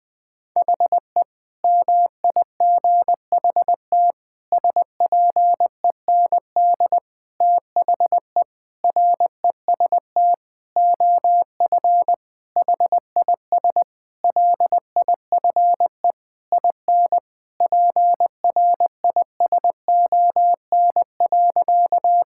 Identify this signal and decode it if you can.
Morse code, heard as a keyed tone, about 20 wpm, 700 Hz. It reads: HE MIGHT SPEND THE REST OF HIS LIFE IN PRISON.
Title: Sentences with 8-Word Phrases - Spoken Beginning - 20wpm